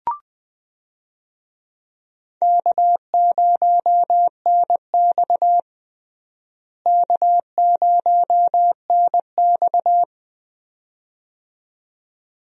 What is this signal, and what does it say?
Morse code, heard as a keyed tone, about 20 wpm, 700 Hz. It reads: K0NX K0NX
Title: USA Call Signs - 4 Characters - Repeated 2x - 20wpm